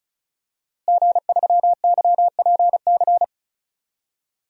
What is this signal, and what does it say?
Morse code, heard as a keyed tone, about 36 wpm, 700 Hz. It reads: G3YPC
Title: DXCC-100 Call Signs - 5 Characters - Repeated 2x - 35wpm